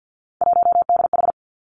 Morse code, heard as a keyed tone, about 50 wpm, 700 Hz. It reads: 1DH
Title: Three Challenging Letter-Number Combinations 50wpm